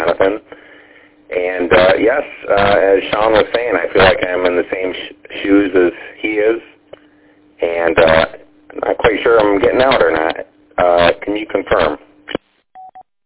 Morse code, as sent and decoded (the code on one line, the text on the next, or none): none